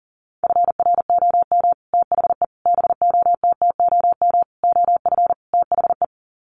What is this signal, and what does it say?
Morse code, heard as a keyed tone, about 40 wpm, 700 Hz. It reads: FROM THE BOTTOM OF THE
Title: Sets of 5 Words Encore2 40wpm